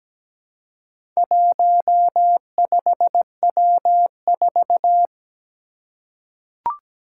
Morse code, heard as a keyed tone, about 17 wpm, 700 Hz. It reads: 15W4
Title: Four Challenging Letter-Number Combinations 17wpm